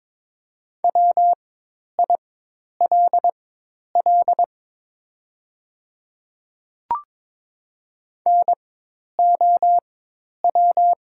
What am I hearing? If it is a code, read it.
Morse code, heard as a keyed tone, about 22 wpm, spaced as for 10 wpm, 700 Hz. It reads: WILL NOW
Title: 4x Character Spacing - Top 100 Words - 22wpm